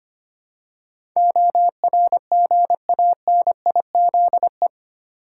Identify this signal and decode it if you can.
Morse code, heard as a keyed tone, about 25 wpm, 700 Hz. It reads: ORGANIZE
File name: 8 Letter Words - First 3 Letters Given - 25wpm